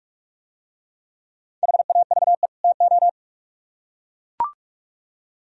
Morse code, heard as a keyed tone, about 41 wpm, 700 Hz. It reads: HAVE TO